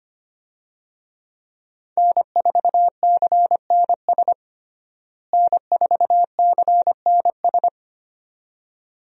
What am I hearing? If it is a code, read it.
Morse code, heard as a keyed tone, about 25 wpm, 700 Hz. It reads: N4CNS N4CNS
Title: USA Call Signs - 5 Characters - Repeated 2x - 25wpm